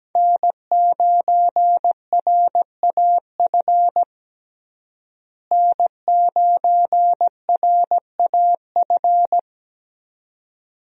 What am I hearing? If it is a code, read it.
Morse code, heard as a keyed tone, about 17 wpm, 700 Hz. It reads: N9RAF N9RAF
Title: USA Call Signs - 5 Characters - Repeated 2x - 17wpm